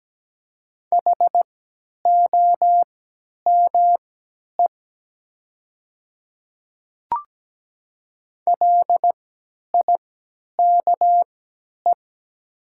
Morse code, heard as a keyed tone, about 17 wpm, spaced as for 10 wpm, 700 Hz. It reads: HOME LIKE
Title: 3x Character Spacing - Top 200 Words - 17wpm